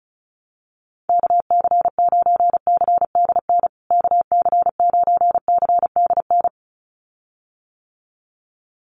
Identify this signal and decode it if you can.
Morse code, heard as a keyed tone, about 34 wpm, 700 Hz. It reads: KC9CDN KC9CDN
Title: USA Call Signs Repeated 2x 35wpm